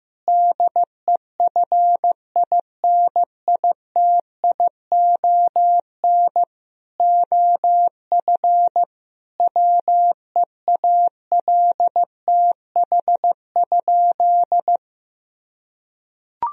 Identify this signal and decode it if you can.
Morse code, heard as a keyed tone, about 15 wpm, 700 Hz. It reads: DEFINITION OF WEALTH?
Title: Sentences from Top 1000 Words - Review Part 4 - 15wpm